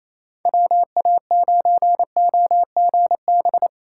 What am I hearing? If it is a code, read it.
Morse code, heard as a keyed tone, about 37 wpm, 700 Hz. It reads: WA9OGB